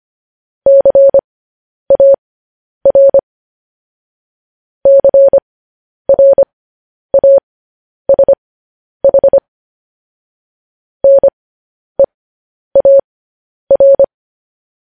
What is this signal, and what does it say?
Morse code, heard as a keyed tone, about 25 wpm, spaced as for 10 wpm, 550 Hz. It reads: CAR CRASH NEAR
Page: news headlines in morse code at 10 wpm